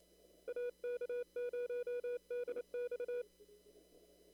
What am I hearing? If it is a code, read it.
Morse code, heard as a keyed tone, about 28 wpm, 490 Hz. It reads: AK0DX
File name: CW / kt5j wpx cw-op-k5pi